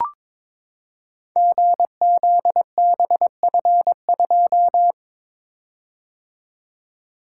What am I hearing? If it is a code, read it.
Morse code, heard as a keyed tone, about 22 wpm, 700 Hz. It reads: GZBF2
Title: Code Groups - 5 Letter:Numbers - 22wpm